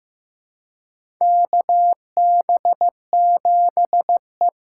Morse code, heard as a keyed tone, about 15 wpm, 700 Hz. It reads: KB7E
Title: USA Call Signs - 2x Character Spacing - 15wpm